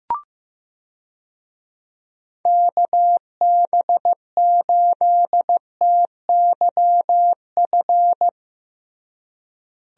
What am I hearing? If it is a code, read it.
Morse code, heard as a keyed tone, about 15 wpm, 700 Hz. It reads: KB8TYF
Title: USA Call Signs - 6 Characters - Repeated 2x - 15wpm